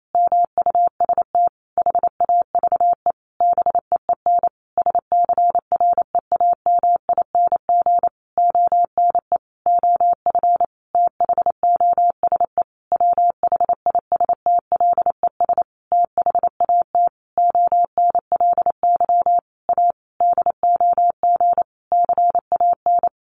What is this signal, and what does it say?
Morse code, heard as a keyed tone, about 28 wpm, 700 Hz. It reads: MUST HAVE BEEN SCREAMING ONE OF THOSE WHISTLES THAT ONLY A DOG CAN
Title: A Sound Of Thunder 28wpm